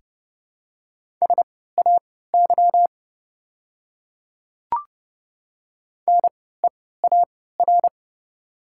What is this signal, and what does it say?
Morse code, heard as a keyed tone, about 31 wpm, spaced as for 17 wpm, 690 Hz. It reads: SAY NEAR
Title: 3x Character Spacing - Top 300 Words - 30wpm